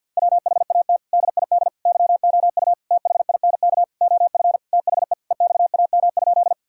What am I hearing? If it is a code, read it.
Morse code, heard as a keyed tone, about 50 wpm, 700 Hz. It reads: WHAT DID YOU THINK OF THE EXAM?
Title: Sentences from Top 1000 Words - Review Part 5 - 50wpm